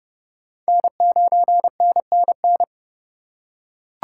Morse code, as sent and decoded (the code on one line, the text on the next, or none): -. ----. -. -. -.
N9NNN